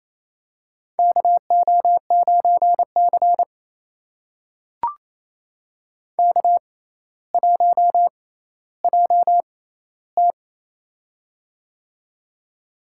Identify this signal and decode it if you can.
Morse code, heard as a keyed tone, about 28 wpm, spaced as for 14 wpm, 700 Hz. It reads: KO9C K1JT
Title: USA Call Signs - 4 Characters - 6x Character Spacing - 28wpm